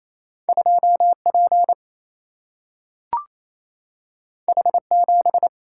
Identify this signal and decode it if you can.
Morse code, heard as a keyed tone, about 28 wpm, 700 Hz. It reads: 2 P H 7
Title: Single-Letter Single-Number 28wpm